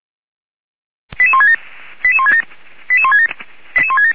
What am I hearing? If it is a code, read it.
Five-tone selective call (ZVEI), heard as quick runs of five tones, tones of about 70 ms. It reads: 88167 88167 88167
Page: SDR/ZVEI